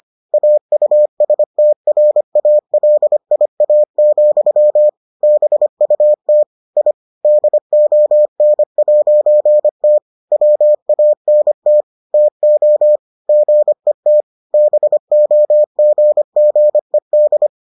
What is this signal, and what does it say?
Morse code, heard as a keyed tone, about 25 wpm, 600 Hz. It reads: AUSTRALIA, BUT I DON'T WANT TO GET BOGGED